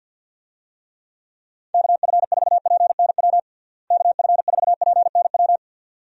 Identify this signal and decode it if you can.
Morse code, heard as a keyed tone, about 50 wpm, 700 Hz. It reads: KF4PNW KF4PNW
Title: USA Call Signs - 6 Characters - Repeated 2x - 50wpm